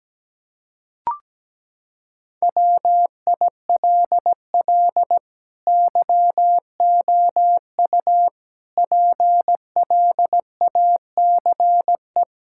Morse code, heard as a keyed tone, about 17 wpm, 700 Hz. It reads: WILL YOU PLACE
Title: Sentences from Top 1600 Words 17wpm